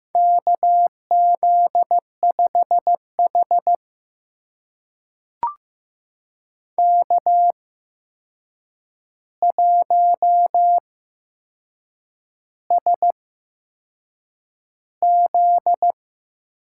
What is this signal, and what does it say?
Morse code, heard as a keyed tone, about 15 wpm, spaced as for 6 wpm, 700 Hz. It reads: KZ5H K1SZ
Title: USA Call Signs - 4 Characters - 8x Character Spacing - 15wpm